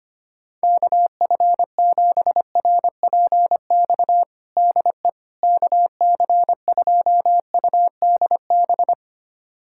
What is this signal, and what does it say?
Morse code, heard as a keyed tone, about 25 wpm, 700 Hz. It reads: KF7RPX DE KC2UDB
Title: USA Callsign DE Callsign - 6 Characters - 25wpm